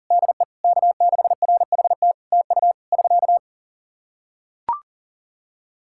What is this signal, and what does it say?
Morse code, heard as a keyed tone, about 40 wpm, 700 Hz. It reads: DE K6RHT TU <SK>
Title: K1USN SST QSO Practice 40wpm